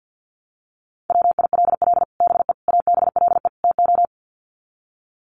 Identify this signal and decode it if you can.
Morse code, heard as a keyed tone, about 48 wpm, 700 Hz. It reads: WILL BE ABLE TO
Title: Sets of 4 Words 50wpm